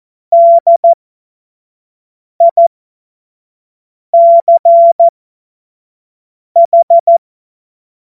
Morse code, heard as a keyed tone, about 14 wpm, spaced as for 5 wpm, 680 Hz. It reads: DICH
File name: Morsekurs / Witze / Witz 30. Dezember 2025 25 F70